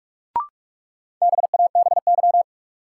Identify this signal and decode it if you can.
Morse code, heard as a keyed tone, about 42 wpm, 700 Hz. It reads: BABY
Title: Sentences from Top 1300 Words - New Word Drill Alt - 45wpm